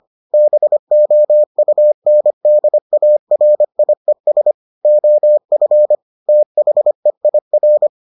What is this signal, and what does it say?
Morse code, heard as a keyed tone, about 25 wpm, 600 Hz. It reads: BOUNDARIES OF THEIR